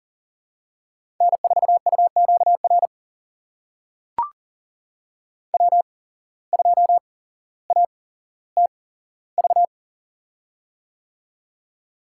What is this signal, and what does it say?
Morse code, heard as a keyed tone, about 40 wpm, spaced as for 17 wpm, 700 Hz. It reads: N4UQR W2ATV
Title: USA Call Signs - 5 Characters - 8x Character Spacing - 40wpm